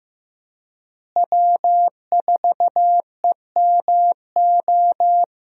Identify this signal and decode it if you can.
Morse code, heard as a keyed tone, about 16 wpm, 700 Hz. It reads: W4EMO
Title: USA Call Signs - 5 Characters - Repeated 2x - 15wpm